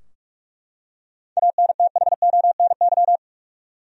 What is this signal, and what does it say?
Morse code, heard as a keyed tone, about 43 wpm, 700 Hz. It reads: ANTHONY
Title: USA Common Names - 45wpm